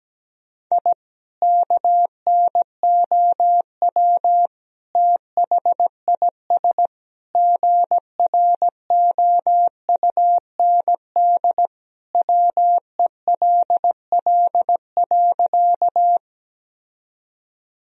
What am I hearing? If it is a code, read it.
Morse code, heard as a keyed tone, about 17 wpm, 700 Hz. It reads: I KNOW THIS GROUND WELL.
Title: Sentences from Top 1000 Words - Review Part 2 - 17wpm